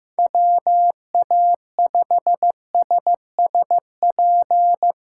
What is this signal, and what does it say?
Morse code, heard as a keyed tone, about 15 wpm, 700 Hz. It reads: WA5SSP
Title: USA Call Signs - 3x Character Spacing - 15wpm